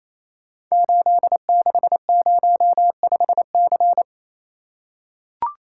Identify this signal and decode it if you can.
Morse code, heard as a keyed tone, about 29 wpm, 700 Hz. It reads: 8605C
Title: Code Groups - 5 Letter:Numbers - 28wpm